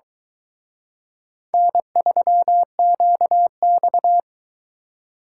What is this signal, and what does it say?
Morse code, heard as a keyed tone, about 22 wpm, 700 Hz. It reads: N3QX